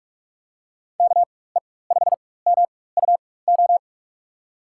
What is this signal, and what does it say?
Morse code, heard as a keyed tone, about 43 wpm, spaced as for 21 wpm, 700 Hz. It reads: KE5MUO
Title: USA Call Signs - 6 Characters - 4x Character Spacing - 45wpm